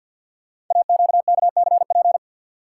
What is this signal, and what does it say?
Morse code, heard as a keyed tone, about 52 wpm, 700 Hz. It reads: AQKCP
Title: Code Groups - 5 Letters - 50wpm